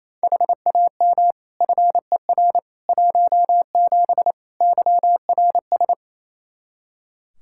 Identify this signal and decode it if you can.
Morse code, heard as a keyed tone, about 28 wpm, 700 Hz. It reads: HAM FER 17 YRS
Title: Practice QSO - Exchange 3 - More Info - 28wpm